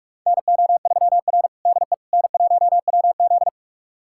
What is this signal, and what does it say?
Morse code, heard as a keyed tone, about 42 wpm, 700 Hz. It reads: NO3R DE N1WZ
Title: USA Callsign DE Callsign - 4 Characters - 45wpm